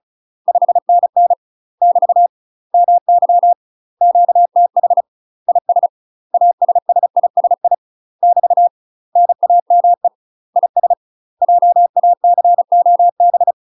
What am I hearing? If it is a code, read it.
Morse code, heard as a keyed tone, about 34 wpm, 700 Hz. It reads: 5NN <BT> MY QTH IS ASSISI <BT> NAME IS JACOB
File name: UK MASTER QSOs TRAINING / 35WPM / UK MASTER QSOs 0004